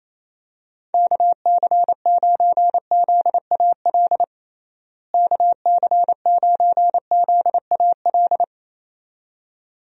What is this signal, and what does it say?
Morse code, heard as a keyed tone, about 28 wpm, 700 Hz. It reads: KC9ZAL KC9ZAL
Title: USA Call Signs - 6 Characters - Repeated 2x - 28wpm